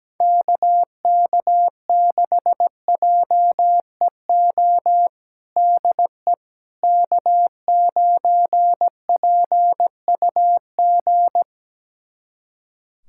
Morse code, heard as a keyed tone, about 17 wpm, 700 Hz. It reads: KK6JEO DE K9PUG
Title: USA Callsign DE Callsign 17wpm